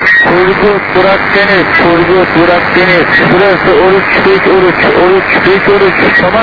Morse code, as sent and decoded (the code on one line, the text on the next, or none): none